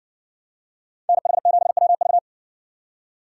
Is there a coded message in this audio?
Yes: Morse code from 1.1 to 2.2 s, heard as a keyed tone, about 60 wpm, 700 Hz. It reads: NH7WV